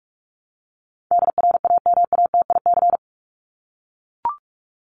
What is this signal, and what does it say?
Morse code, heard as a keyed tone, about 44 wpm, 700 Hz. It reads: DRAMATIC
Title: Top 1251-1500 Words 45wpm